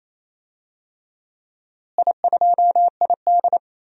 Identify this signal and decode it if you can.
Morse code, heard as a keyed tone, about 28 wpm, 700 Hz. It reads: I2ID
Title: DXCC-100 Call Signs - 4 Characters - Repeated 3x - 28wpm